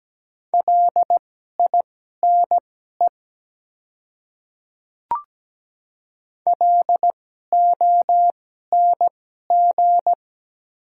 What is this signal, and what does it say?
Morse code, heard as a keyed tone, about 17 wpm, spaced as for 12 wpm, 700 Hz. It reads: LINE LONG
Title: 2x Character Spacing - Top 300 Words - 17wpm